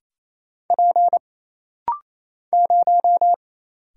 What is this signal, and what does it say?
Morse code, heard as a keyed tone, about 27 wpm, 700 Hz. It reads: P 0
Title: Single Letter-Number Alt - Rapid-Fire 28wpm